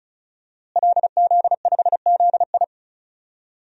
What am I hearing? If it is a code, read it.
Morse code, heard as a keyed tone, about 36 wpm, 700 Hz. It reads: LZ5ZI
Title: DXCC-100 Call Signs - 5 Characters - Repeated 2x - 35wpm